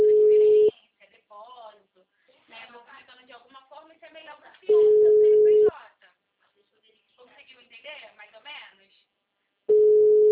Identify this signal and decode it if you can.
Ringback tone (Europe, 425 Hz), with speech.